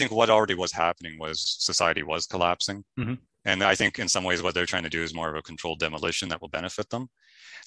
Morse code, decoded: none